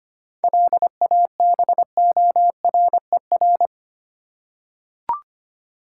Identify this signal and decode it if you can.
Morse code, heard as a keyed tone, about 25 wpm, 700 Hz. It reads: LABORER
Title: Practice QSO - Exchange 3 - More Info - 25wpm